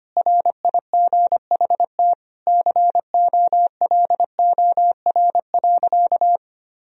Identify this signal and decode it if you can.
Morse code, heard as a keyed tone, about 25 wpm, 700 Hz. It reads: RIGHT COLOR.
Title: Sentences from Top 1000 Words - Review Part 2 - 25wpm